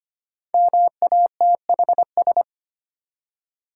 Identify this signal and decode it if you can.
Morse code, heard as a keyed tone, about 25 wpm, 700 Hz. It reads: MATHS
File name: 5 Letter Words - First 3 Letters Given - 25wpm